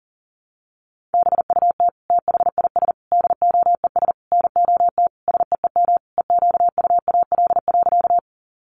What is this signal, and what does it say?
Morse code, heard as a keyed tone, about 40 wpm, 700 Hz. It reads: BUT THIS DOES NOT SEEM EQUAL.
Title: Sentences from Top 1400 Words 40wpm